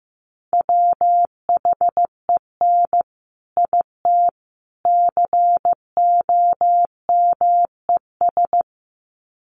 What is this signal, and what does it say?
Morse code, heard as a keyed tone, about 15 wpm, 700 Hz. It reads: WHEN IT COMES
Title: Sets of 3 Words 15wpm